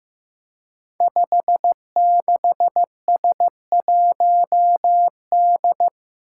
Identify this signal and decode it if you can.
Morse code, heard as a keyed tone, about 15 wpm, 700 Hz. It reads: 56S1D
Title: Five Challenging Letter-Number Combinations 15wpm